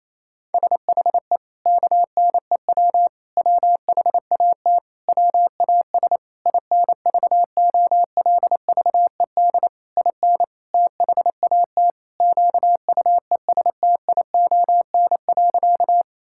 Morse code, heard as a keyed tone, about 28 wpm, 700 Hz. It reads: SHE KNEW WHAT WAS INVOLVED IN THAT QUESTION.
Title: Sentences from Top 1000 Words - Review Part 5 - 28wpm